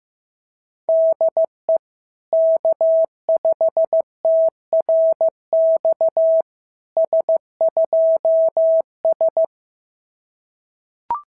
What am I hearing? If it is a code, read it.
Morse code, heard as a keyed tone, about 16 wpm, 690 Hz. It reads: DE K5TRX S2S
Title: SOTA QSO Practice 15wpm